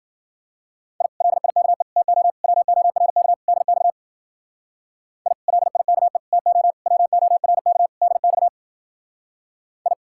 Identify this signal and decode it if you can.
Morse code, heard as a keyed tone, about 48 wpm, 700 Hz. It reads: I LIKE TO WORK DX I LIKE TO WORK DX I
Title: Speed-Racing Common QSO Phrases - 40wpm